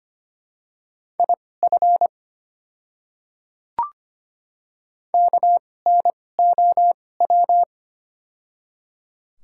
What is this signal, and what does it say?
Morse code, heard as a keyed tone, about 25 wpm, spaced as for 18 wpm, 700 Hz. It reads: IF KNOW
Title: 2x Character Spacing - Top 100 Words - 25wpm